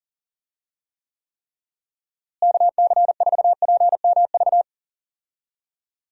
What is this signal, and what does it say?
Morse code, heard as a keyed tone, about 40 wpm, 700 Hz. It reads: KC4PMV
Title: USA Call Signs - 6 Characters - Repeated 3x - 40wpm